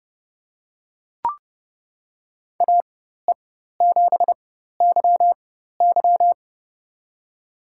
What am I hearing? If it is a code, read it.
Morse code, heard as a keyed tone, about 30 wpm, spaced as for 14 wpm, 700 Hz. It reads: AE7YY
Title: USA Call Signs - 4x Character Spacing - 30wpm